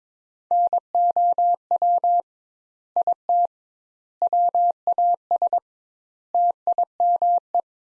Morse code, heard as a keyed tone, about 21 wpm, 700 Hz. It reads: NOW IT WAS TIME
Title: Sentences Top 200 Words - 22wpm